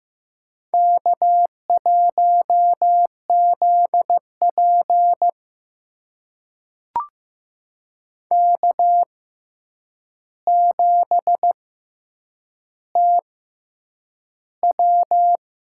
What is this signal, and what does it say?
Morse code, heard as a keyed tone, about 15 wpm, spaced as for 8 wpm, 700 Hz. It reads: K1ZP K7TW
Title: USA Call Signs - 4 Characters - 6x Character Spacing - 15wpm